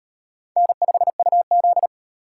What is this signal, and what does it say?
Morse code, heard as a keyed tone, about 36 wpm, 700 Hz. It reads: N5UZ